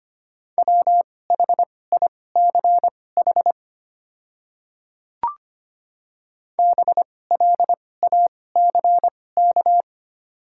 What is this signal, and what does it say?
Morse code, heard as a keyed tone, about 25 wpm, spaced as for 18 wpm, 700 Hz. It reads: WHICH BLACK
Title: 2x Character Spacing - Top 500 Words - 25wpm